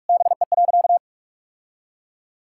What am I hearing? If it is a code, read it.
Morse code, heard as a keyed tone, about 43 wpm, 700 Hz. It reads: BE.
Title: Sentences from Top 1000 Words - Review Part 6 - 45wpm